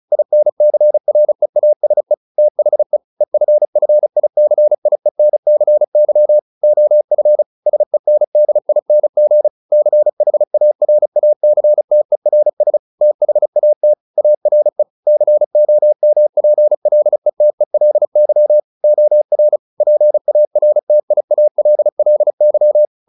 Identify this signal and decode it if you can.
Morse code, heard as a keyed tone, about 34 wpm, 600 Hz. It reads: INCREASE THE EFFICIENCY OF SENDING CHARACTERS THAT ARE COMPLETELY OR PARTIALLY